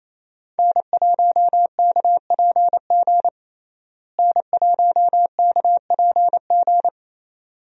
Morse code, heard as a keyed tone, about 28 wpm, 700 Hz. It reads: N1KPG N1KPG
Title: USA Call Signs - 5 Characters - Repeated 2x - 28wpm